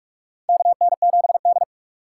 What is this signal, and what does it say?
Morse code, heard as a keyed tone, about 46 wpm, 700 Hz. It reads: KN7D
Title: K1USN SST QSO Practice - 45wpm